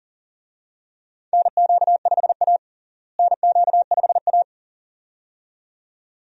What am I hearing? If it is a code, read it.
Morse code, heard as a keyed tone, about 40 wpm, 700 Hz. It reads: NQ5A NQ5A
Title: USA Call Signs - 4 Characters - Repeated 2x - 40wpm